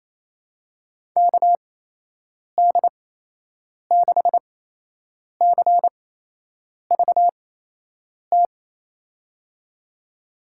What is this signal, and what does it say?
Morse code, heard as a keyed tone, about 27 wpm, spaced as for 8 wpm, 700 Hz. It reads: KD6CVT
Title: USA Call Signs - 6 Characters - 8x Character Spacing - 28wpm